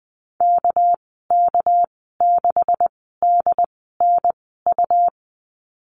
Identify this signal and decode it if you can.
Morse code, heard as a keyed tone, about 20 wpm, spaced as for 14 wpm, 700 Hz. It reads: KK6DNU
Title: 207 Lesson 33g - Letter j - Call Signs 20wpm